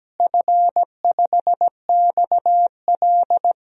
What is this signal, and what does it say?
Morse code, heard as a keyed tone, about 17 wpm, 700 Hz. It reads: F5XL
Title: DXCC-100 Call Signs - 4 Characters - Repeated 3x - 17wpm